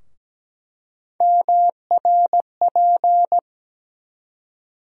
Morse code, heard as a keyed tone, about 17 wpm, 700 Hz. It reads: MRP
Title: USA Call Sign Postfixes - 3 Letters 17wpm